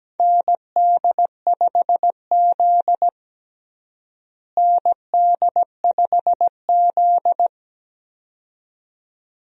Morse code, heard as a keyed tone, about 17 wpm, 700 Hz. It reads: ND5Z ND5Z
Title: USA Call Signs - 4 Characters - Repeated 2x - 17wpm